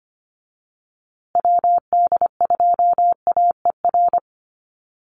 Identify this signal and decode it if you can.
Morse code, heard as a keyed tone, about 25 wpm, 700 Hz. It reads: WD2AER
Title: USA Call Signs Repeated 3x 25wpm